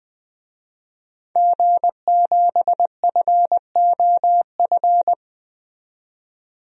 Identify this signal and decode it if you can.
Morse code, heard as a keyed tone, about 20 wpm, 700 Hz. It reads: G7FOF
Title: DXCC-100 Call Signs - 5 Characters - 20wpm